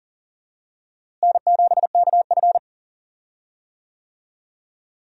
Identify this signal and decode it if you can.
Morse code, heard as a keyed tone, about 40 wpm, 700 Hz. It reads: N7KF
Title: USA Call Signs - 4 Characters - Repeated 3x - 40wpm